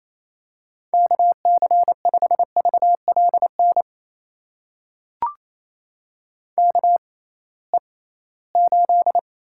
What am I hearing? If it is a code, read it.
Morse code, heard as a keyed tone, about 28 wpm, spaced as for 18 wpm, 700 Hz. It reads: KC5VLN KE8
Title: USA Call Signs - 6 Characters - 6x Character Spacing - 28wpm